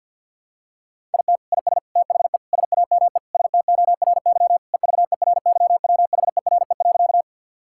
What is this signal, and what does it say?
Morse code, heard as a keyed tone, about 50 wpm, 700 Hz. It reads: IT IS THE SAME STORY EVERYWHERE.
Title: Sentences from Top 1600 Words 50wpm